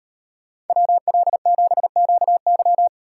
Morse code, heard as a keyed tone, about 38 wpm, 700 Hz. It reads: WL7QY